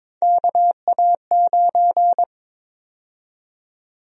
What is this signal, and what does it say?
Morse code, heard as a keyed tone, about 22 wpm, 700 Hz. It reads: KA9
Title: USA Call Sign Prefixes - Repeated 2x - 22wpm